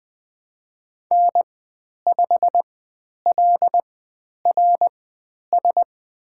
Morse code, heard as a keyed tone, about 20 wpm, spaced as for 10 wpm, 700 Hz. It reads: N5LRS
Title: 095 Lesson 16g - Letter c - Call Signs 20wpm